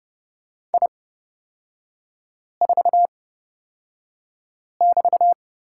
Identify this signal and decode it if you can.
Morse code, heard as a keyed tone, about 31 wpm, spaced as for 5 wpm, 700 Hz. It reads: I4<BT>
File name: Lesson 36e - BT - Three Characters 30wpm